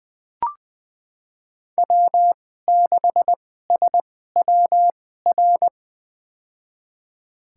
Morse code, heard as a keyed tone, about 20 wpm, spaced as for 14 wpm, 700 Hz. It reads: W6SWR
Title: USA Call Signs - 5 Characters - 2x Character Spacing - 20wpm